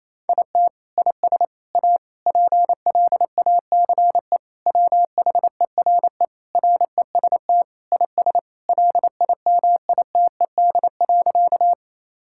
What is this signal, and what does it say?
Morse code, heard as a keyed tone, about 28 wpm, 700 Hz. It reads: IT IS A PLACE WHERE REST IS LIMITED.
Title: Sentences from Top 1000 Words - Review Part 6 - 28wpm